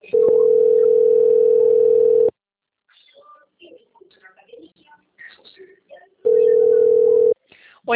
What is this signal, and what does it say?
Ringback tone (North America), 440 + 480 Hz.